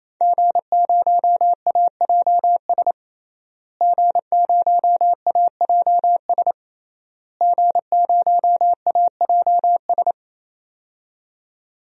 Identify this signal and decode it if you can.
Morse code, heard as a keyed tone, about 28 wpm, 700 Hz. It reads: G0AJS G0AJS G0AJS
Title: DXCC-100 Call Signs - 5 Characters - Repeated 3x - 28wpm